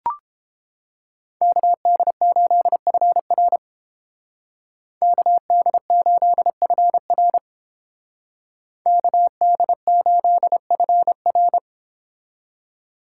Morse code, heard as a keyed tone, about 30 wpm, 700 Hz. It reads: KD8FR KD8FR KD8FR